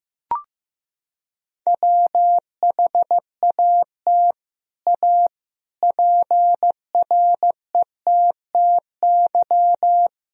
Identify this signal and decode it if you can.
Morse code, heard as a keyed tone, about 15 wpm, 700 Hz. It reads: WHAT A PRETTY
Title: Sentences from Top 1000 Words - Review Part 2 - 15wpm